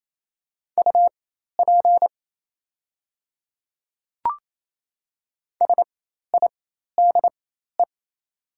Morse code, heard as a keyed tone, about 28 wpm, spaced as for 13 wpm, 700 Hz. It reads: UP SIDE
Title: 4x Character Spacing - Top 100 Words - 28wpm